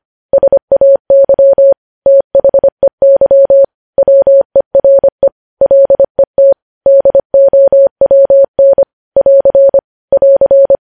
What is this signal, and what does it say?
Morse code, heard as a keyed tone, about 25 wpm, 550 Hz. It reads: SAY THEY WERE LET DOWN <AR> <AR>
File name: podcast / News headlines-1x-25x25